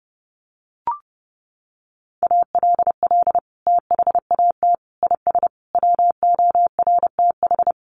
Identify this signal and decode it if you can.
Morse code, heard as a keyed tone, about 30 wpm, 700 Hz. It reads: ALL THAT IS WORTH
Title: Sentences from Top 1100 Words 30wpm